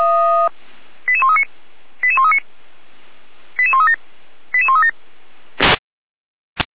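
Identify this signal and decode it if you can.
Five-tone selective call (ZVEI), heard as quick runs of five tones, tones of about 70 ms. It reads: 88139 88139 88137 88137
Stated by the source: ZVEI five-tone call 88335 received by SDR, from audio.f85360.feuerwehr-orlamuende.org, a fire brigade (Feuerwehr Orlamünde) archive.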